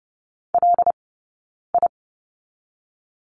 Morse code, heard as a keyed tone, about 30 wpm, spaced as for 9 wpm, 700 Hz. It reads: LI
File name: Lesson 20d - Number 3 - Two Characters 30wpm